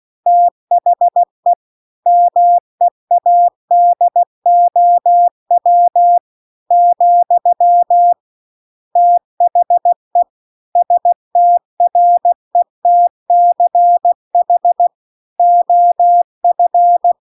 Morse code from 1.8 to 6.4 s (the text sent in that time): MEADOW